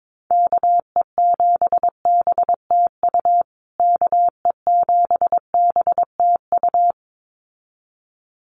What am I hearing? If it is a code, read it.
Morse code, heard as a keyed tone, about 22 wpm, 700 Hz. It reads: KE7BTU KE7BTU
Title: USA Call Signs Repeated 2x 22wpm